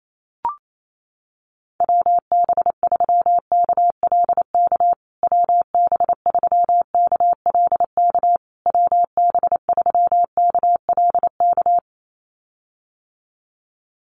Morse code, heard as a keyed tone, about 28 wpm, 700 Hz. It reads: WB3KLK WB3KLK WB3KLK
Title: USA Call Signs Repeated 3x 28wpm